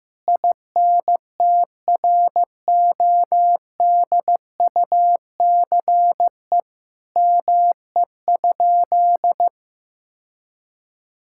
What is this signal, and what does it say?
Morse code, heard as a keyed tone, about 15 wpm, 700 Hz. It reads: INTRODUCE ME?
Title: Sentences from Top 1600 Words 15wpm